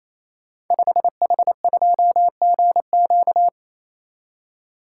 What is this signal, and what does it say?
Morse code, heard as a keyed tone, about 27 wpm, 690 Hz. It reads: 5H2GQ
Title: Code Groups - 5 Letter:Numbers - 28wpm